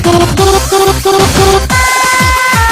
There is no code.